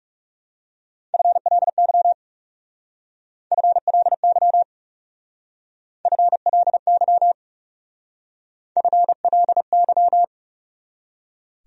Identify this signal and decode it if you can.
Morse code, heard as a keyed tone, about 37 wpm, 700 Hz. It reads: FLY FLY FLY FLY
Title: Speed-Racing Top 500 Words 30wpm